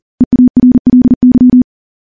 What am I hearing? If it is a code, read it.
Morse code, heard as a keyed tone, about 41 wpm, 250 Hz. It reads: EARLY
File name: morse / morse-training-40WPM